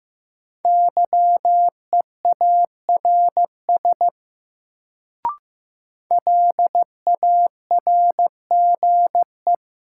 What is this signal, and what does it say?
Morse code, heard as a keyed tone, about 15 wpm, 700 Hz. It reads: YEARS LARGE
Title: Sentences from Top 200 Words - New Word Drill - 15wpm